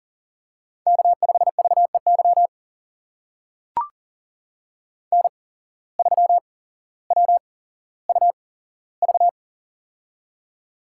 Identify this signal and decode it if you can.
Morse code, heard as a keyed tone, about 38 wpm, spaced as for 17 wpm, 700 Hz. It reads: K5VEY N3WUV
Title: USA Call Signs - 5 Characters - 8x Character Spacing - 40wpm